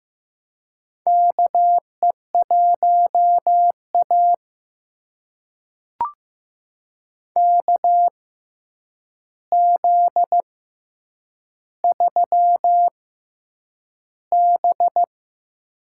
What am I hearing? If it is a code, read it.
Morse code, heard as a keyed tone, about 15 wpm, spaced as for 8 wpm, 700 Hz. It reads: KE1A KZ3B
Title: USA Call Signs - 4 Characters - 6x Character Spacing - 15wpm